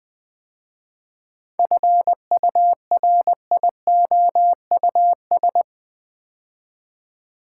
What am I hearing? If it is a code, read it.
Morse code, heard as a keyed tone, about 20 wpm, 700 Hz. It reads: FURIOUS
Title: 7 Letter Words - First 3 Letters Given - 20wpm